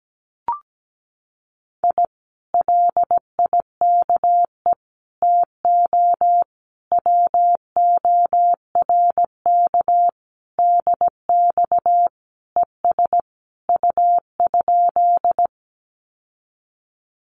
Morse code, heard as a keyed tone, about 17 wpm, 700 Hz. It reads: I LIKE TO WORK DX ES U?
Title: Common QSO Phrases Encore 17wpm